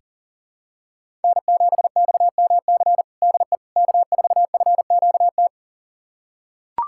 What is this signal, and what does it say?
Morse code, heard as a keyed tone, about 37 wpm, 700 Hz. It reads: N7XMC DE K4FQT
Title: USA Callsign DE Callsign 40wpm